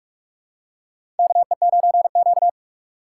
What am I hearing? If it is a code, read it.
Morse code, heard as a keyed tone, about 43 wpm, 700 Hz. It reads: KE9Q
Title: USA Call Signs - 4 Characters - 3x Character Spacing - 45wpm